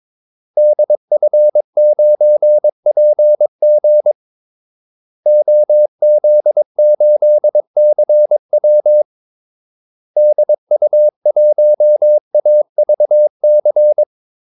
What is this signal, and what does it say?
Morse code, heard as a keyed tone, about 22 wpm, 600 Hz. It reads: DF9PG OZ8CW DU1AVC
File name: Callsigns-22wpm